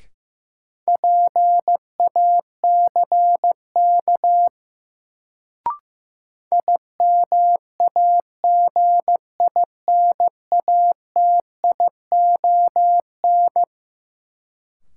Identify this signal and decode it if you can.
Morse code, heard as a keyed tone, about 15 wpm, 700 Hz. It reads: PACK IMAGINATION
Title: Sentences from Top 1300 Words - New Word Drill Alt - 15wpm